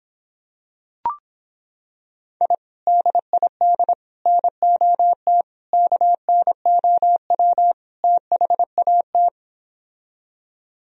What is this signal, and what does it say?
Morse code, heard as a keyed tone, about 26 wpm, 700 Hz. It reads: I DID NOT KNOW THAT